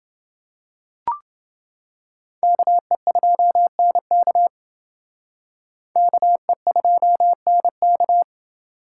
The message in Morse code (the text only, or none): KE2NK KE2NK